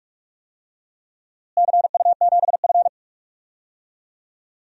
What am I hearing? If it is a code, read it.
Morse code, heard as a keyed tone, about 46 wpm, 700 Hz. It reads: CU7F
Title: DXCC-100 Call Signs - 4 Characters - Repeated 2x - 45wpm